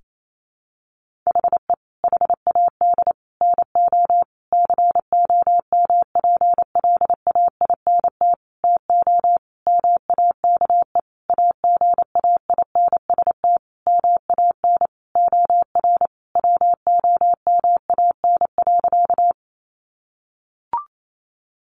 Morse code, heard as a keyed tone, about 28 wpm, 700 Hz. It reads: HE HAD NO COMPLAINT TO MAKE AGAINST MAN OR WOMAN.
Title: Sentences from Top 1400 Words 28wpm